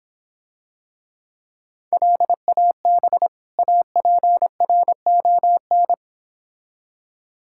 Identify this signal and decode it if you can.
Morse code, heard as a keyed tone, about 25 wpm, 700 Hz. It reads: LAB APRON